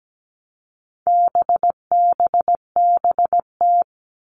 Morse code, heard as a keyed tone, about 17 wpm, 700 Hz. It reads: BBBT
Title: All Cut Numbers - 4-Digit 17wpm